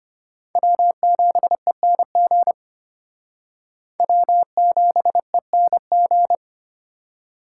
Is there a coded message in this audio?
Yes: Morse code from 0.6 to 6.3 s, heard as a keyed tone, about 31 wpm, 700 Hz. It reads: W7ENG W7ENG